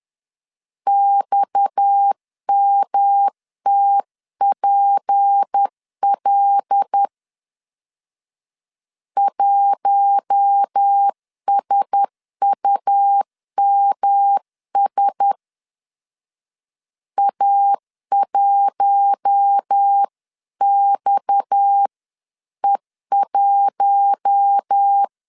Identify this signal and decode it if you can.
Morse code, heard as a keyed tone, about 10 wpm, 800 Hz. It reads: XMTPL 1SUMS A1XE1
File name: CW / DET / 1 6 1500C S C 10wpm 4252